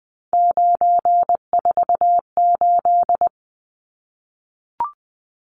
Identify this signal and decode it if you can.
Morse code, heard as a keyed tone, about 20 wpm, 700 Hz. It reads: 948
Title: Numbers - 3-Digit 20wpm